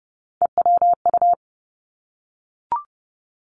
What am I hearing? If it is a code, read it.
Morse code, heard as a keyed tone, about 30 wpm, 700 Hz. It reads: EWU
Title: Three Challenging Letter-Number Combinations 30wpm